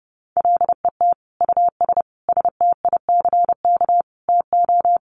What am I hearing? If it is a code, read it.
Morse code, heard as a keyed tone, about 30 wpm, 700 Hz. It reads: LET US STICK TO
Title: Sentences from Top 1200 Words 30wpm